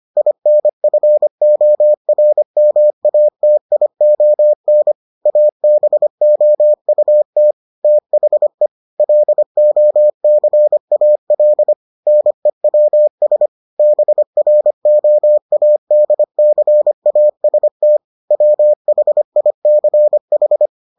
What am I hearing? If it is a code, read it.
Morse code, heard as a keyed tone, about 25 wpm, 600 Hz. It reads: INFORMATION ABOUT THE LOCAL NEWS BROADCAST WHICH